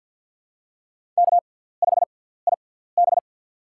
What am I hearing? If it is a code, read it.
Morse code, heard as a keyed tone, about 49 wpm, spaced as for 17 wpm, 700 Hz. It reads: K5IB
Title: USA Call Signs - 4 Characters - 6x Character Spacing - 50wpm